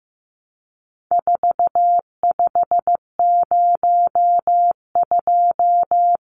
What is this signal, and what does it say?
Morse code, heard as a keyed tone, about 15 wpm, 700 Hz. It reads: 4502
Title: Numbers - 4-Digit 15wpm